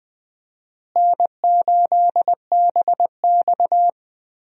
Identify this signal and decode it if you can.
Morse code, heard as a keyed tone, about 20 wpm, 700 Hz. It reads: N8BX
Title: USA Call Signs - 4 Characters - Repeated 3x - 20wpm